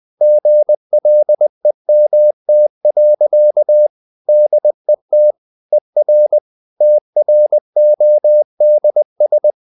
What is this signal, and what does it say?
Morse code, heard as a keyed tone, about 20 wpm, 600 Hz. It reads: GLEMT. DET ER TRODS
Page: Ferie 20 wpm